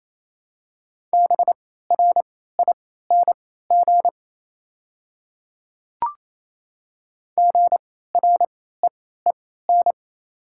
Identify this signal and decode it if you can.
Morse code, heard as a keyed tone, about 27 wpm, spaced as for 16 wpm, 700 Hz. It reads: BRING GREEN
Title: 3x Character Spacing - Top 500 Words - 28wpm